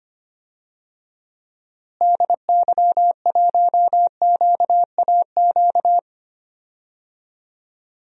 Morse code, heard as a keyed tone, about 25 wpm, 700 Hz. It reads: DY1QAQ
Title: DXCC-100 Call Signs - 6 Characters - Repeated 2x - 25wpm